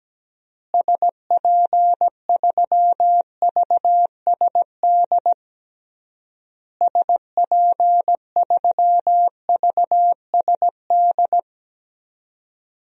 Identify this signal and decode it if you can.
Morse code, heard as a keyed tone, about 17 wpm, 700 Hz. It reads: SP3VSD SP3VSD
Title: DXCC-100 Call Signs - 6 Characters - Repeated 3x - 17wpm